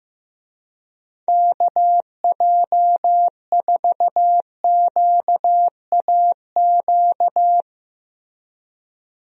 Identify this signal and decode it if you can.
Morse code, heard as a keyed tone, about 15 wpm, 700 Hz. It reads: KJ4QAQ
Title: USA Call Signs - 6 Characters - Repeated 2x - 15wpm